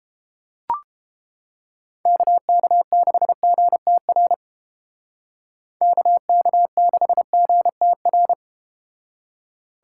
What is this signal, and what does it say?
Morse code, heard as a keyed tone, about 31 wpm, 700 Hz. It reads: KK6GTR KK6GTR